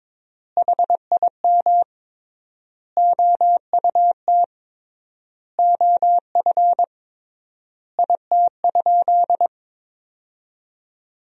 Morse code, heard as a keyed tone, about 22 wpm, 700 Hz. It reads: HIM OUT OF IT?
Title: 3x Word-Spacing - Sentences from Top 300 Words - 22wpm